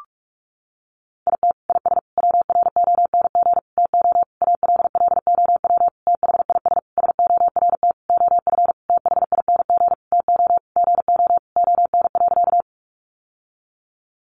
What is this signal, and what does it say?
Morse code, heard as a keyed tone, about 44 wpm, 700 Hz. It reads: IT IS WRONG TO ALLOW THIS SORT OF THING TO GO ON.
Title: Sentences from Top 1300 Words 45wpm